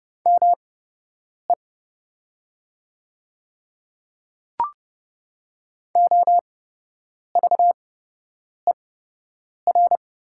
Morse code, heard as a keyed tone, about 29 wpm, spaced as for 8 wpm, 690 Hz. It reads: ME OVER
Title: 8x Character Spacing - Top 300 Words - 30wpm